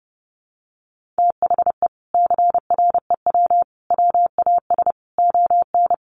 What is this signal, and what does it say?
Morse code, heard as a keyed tone, about 30 wpm, 700 Hz. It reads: THE CREW WAS ON
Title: Sentences from Top 1500 Words 30wpm